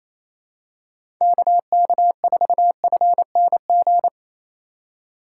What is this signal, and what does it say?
Morse code, heard as a keyed tone, about 27 wpm, 700 Hz. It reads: KK4FNG
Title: USA Call Signs - 6 Characters - 3x Character Spacing - 28wpm